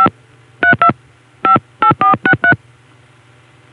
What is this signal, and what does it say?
DTMF, heard as telephone touch tones, tones of about 100 ms. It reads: 5 32 58463